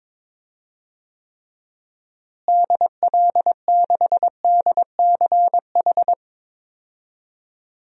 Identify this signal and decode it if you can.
Morse code, heard as a keyed tone, about 21 wpm, 700 Hz. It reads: DL6DCH
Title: DXCC-100 Call Signs - 6 Characters - Repeated 3x - 22wpm